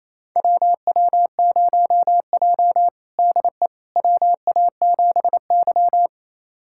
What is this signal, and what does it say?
Morse code, heard as a keyed tone, about 28 wpm, 700 Hz. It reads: WW0J DE WA7Y
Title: USA Callsign DE Callsign - 4 Characters - 28wpm